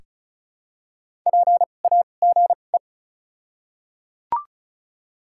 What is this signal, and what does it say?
Morse code, heard as a keyed tone, about 34 wpm, spaced as for 25 wpm, 700 Hz. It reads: PAGE
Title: 2x Character Spacing - Top 300 Words - 35wpm